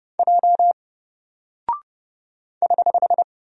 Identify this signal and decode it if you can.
Morse code, heard as a keyed tone, about 30 wpm, 700 Hz. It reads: J <HH>
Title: Single Character (Letter:Number:Punctuation:Prosign) Alt 30wpm